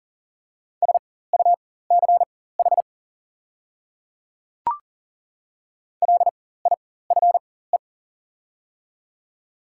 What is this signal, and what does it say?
Morse code, heard as a keyed tone, about 40 wpm, spaced as for 19 wpm, 690 Hz. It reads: SUCH LIFE
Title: 4x Character Spacing - Top 300 Words - 40wpm